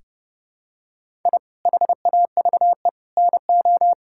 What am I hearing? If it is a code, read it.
Morse code, heard as a keyed tone, about 32 wpm, 700 Hz. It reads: I HAVE NO
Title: Sentences from Top 1000 Words - Review Part 6 - 30wpm